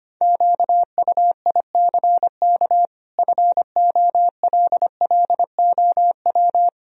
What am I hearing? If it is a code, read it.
Morse code, heard as a keyed tone, about 25 wpm, 700 Hz. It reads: QUICK FOLLOW
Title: Sentences from Top 1000 Words - Review Part 1 - 25wpm